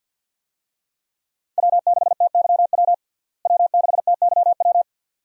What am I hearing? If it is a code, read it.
Morse code, heard as a keyed tone, about 51 wpm, 700 Hz. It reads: W6TYW W6TYW
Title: USA Call Signs - 5 Characters - Repeated 2x - 50wpm